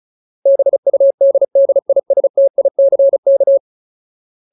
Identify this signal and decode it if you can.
Morse code, heard as a keyed tone, about 35 wpm, 550 Hz. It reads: BUDDISTICK
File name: Ultimate Word List / 35WPM Ultimate